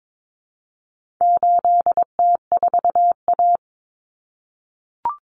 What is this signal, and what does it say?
Morse code, heard as a keyed tone, about 22 wpm, 700 Hz. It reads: 8T4A
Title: Common Cut Numbers - 4-Digit 22wpm